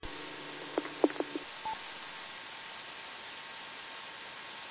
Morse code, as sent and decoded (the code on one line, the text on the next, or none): none